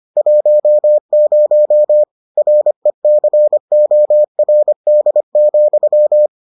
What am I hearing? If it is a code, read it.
Morse code, heard as a keyed tone, about 25 wpm, 600 Hz. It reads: 10 RECORD,